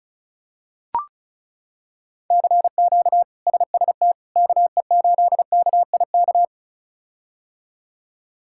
Morse code, heard as a keyed tone, about 34 wpm, 700 Hz. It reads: CQ SST KE8KIK
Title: K1USN SST QSO Practice - 35wpm